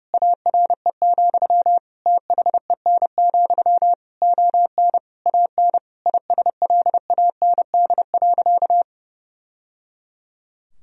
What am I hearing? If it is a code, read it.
Morse code, heard as a keyed tone, about 30 wpm, 700 Hz. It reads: ARE, THEN, ON AN ISLAND.
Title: Sentences from Top 1000 Words - Review Part 3 - 30wpm